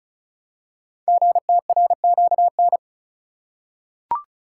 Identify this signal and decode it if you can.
Morse code, heard as a keyed tone, about 35 wpm, 700 Hz. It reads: GTRQN
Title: Code Groups - 5 Letters - 35wpm